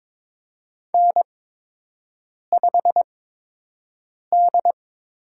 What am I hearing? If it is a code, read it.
Morse code, heard as a keyed tone, about 22 wpm, spaced as for 6 wpm, 700 Hz. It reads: N5D
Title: USA Call Signs - 5 Characters - 8x Character Spacing - 22wpm